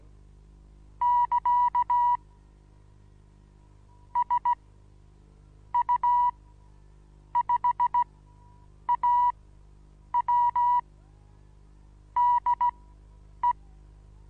Morse code, heard as a keyed tone, about 17 wpm, spaced as for 7 wpm, 970 Hz. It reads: <KA> SU5AW DE
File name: Onlinelehrgang / morsen / DJ4UF Morsen25